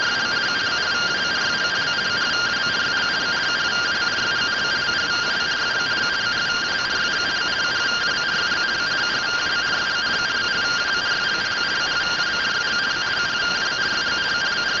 Two-tone teleprinter signal, about 170 Hz shift.